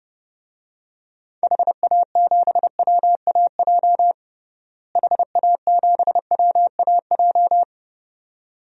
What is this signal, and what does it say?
Morse code, heard as a keyed tone, about 30 wpm, 700 Hz. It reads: HA7WAJ HA7WAJ
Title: DXCC-100 Call Signs - 6 Characters - Repeated 2x - 30wpm